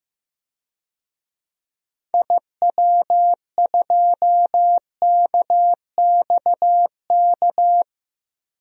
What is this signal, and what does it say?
Morse code, heard as a keyed tone, about 15 wpm, 690 Hz. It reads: IW2KXK
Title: DXCC-100 Call Signs - 6 Characters - 15wpm